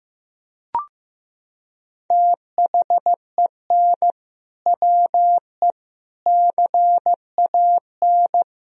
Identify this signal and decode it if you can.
Morse code, heard as a keyed tone, about 15 wpm, 700 Hz. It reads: THEN WE CAN
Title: Sentences from Top 1000 Words - Review Part 2 - 15wpm